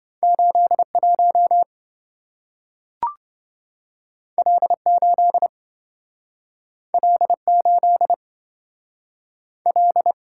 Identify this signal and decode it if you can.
Morse code, heard as a keyed tone, about 27 wpm, 700 Hz. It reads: 81 L8 L8 L